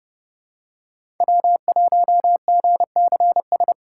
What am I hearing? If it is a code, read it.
Morse code, heard as a keyed tone, about 30 wpm, 700 Hz. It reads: W1GCS
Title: K1USN SST QSO Practice - 30wpm